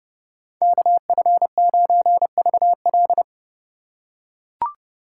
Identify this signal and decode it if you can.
Morse code, heard as a keyed tone, about 30 wpm, 700 Hz. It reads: KF9VL